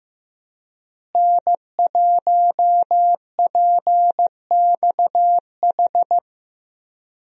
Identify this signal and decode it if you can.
Morse code, heard as a keyed tone, about 15 wpm, 700 Hz. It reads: N1PXH
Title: USA Call Signs - 5 Characters - Repeated 3x - 15wpm